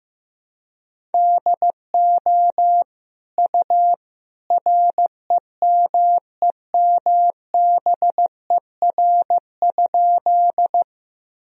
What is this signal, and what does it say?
Morse code, heard as a keyed tone, about 15 wpm, 700 Hz. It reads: DO U REMEMBER?
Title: Sentences from Top 400 Words and Abbreviations 15wpm